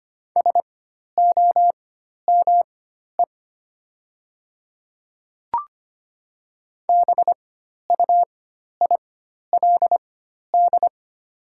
Morse code, heard as a keyed tone, about 25 wpm, spaced as for 12 wpm, 700 Hz. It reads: SOME BUILD
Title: 4x Character Spacing - Top 300 Words - 25wpm